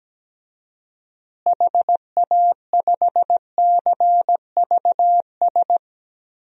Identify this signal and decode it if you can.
Morse code, heard as a keyed tone, about 17 wpm, 700 Hz. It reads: HA5CVS
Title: DXCC-100 Call Signs - 6 Characters - Repeated 2x - 17wpm